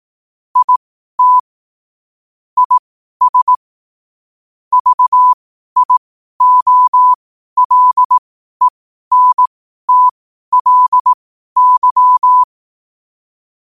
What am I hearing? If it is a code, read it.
Morse code, heard as a keyed tone, about 18 wpm, spaced as for 12 wpm, 1000 Hz. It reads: IT IS VIOLENTLY
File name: qotdfiles / 12wpm